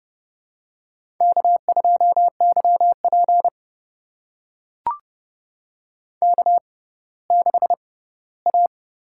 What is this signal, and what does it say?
Morse code, heard as a keyed tone, about 30 wpm, spaced as for 17 wpm, 700 Hz. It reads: K2YP K6A
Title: USA Call Signs - 6x Character Spacing - 30wpm